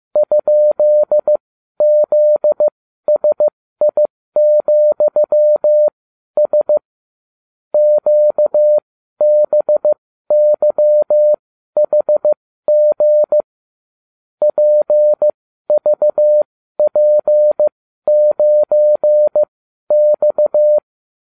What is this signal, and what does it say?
Morse code, heard as a keyed tone, about 15 wpm, spaced as for 12 wpm, 600 Hz. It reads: ?ZSI,S QBYHG PVP9X